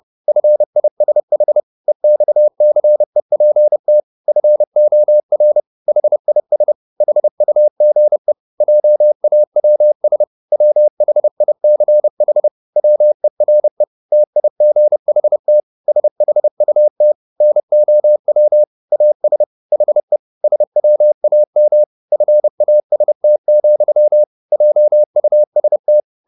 Morse code, heard as a keyed tone, about 30 wpm, 600 Hz. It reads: FISH EXCEPT FOR HIS HUGE JAWS WHICH WERE TIGHT SHUT NOW AS HE SWAM FAST, JUST